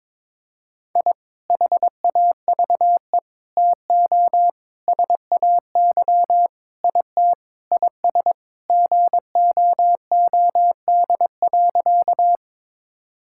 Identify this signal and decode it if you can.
Morse code, heard as a keyed tone, about 22 wpm, 700 Hz. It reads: I HAVE TO SAY IT IS GOOD.
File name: practice / 22wpm / Sentences with 4-Word Phrases - 22wpm